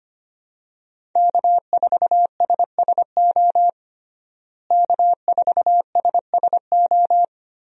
Morse code, heard as a keyed tone, about 25 wpm, 700 Hz. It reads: K4SSO K4SSO
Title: USA Call Signs - 5 Characters - Repeated 2x - 25wpm